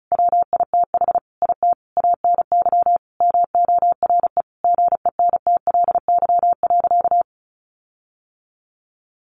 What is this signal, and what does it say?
Morse code, heard as a keyed tone, about 34 wpm, 700 Hz. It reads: WITH IT ANY MORE GENTLY.
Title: Sentences from Top 1500 Words 35wpm